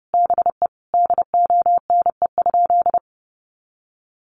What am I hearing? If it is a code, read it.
Morse code, heard as a keyed tone, about 30 wpm, 700 Hz. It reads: BE DONE?
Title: Sentences from Top 500 Words 30wpm